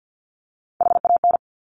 Morse code, heard as a keyed tone, about 50 wpm, 700 Hz. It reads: HAN
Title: Top Three Letter Combinations in Words 50wpm